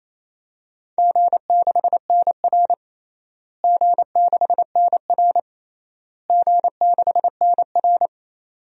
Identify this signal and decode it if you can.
Morse code, heard as a keyed tone, about 28 wpm, 700 Hz. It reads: G6NR G6NR G6NR
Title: DXCC-100 Call Signs - 4 Characters - Repeated 3x - 28wpm